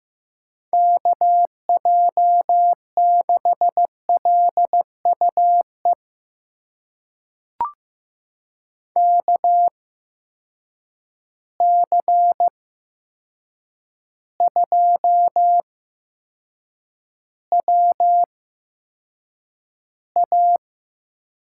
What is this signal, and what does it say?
Morse code, heard as a keyed tone, about 15 wpm, spaced as for 7 wpm, 700 Hz. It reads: KJ6LUE KC2WA
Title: USA Call Signs - 6 Characters - 8x Character Spacing - 15wpm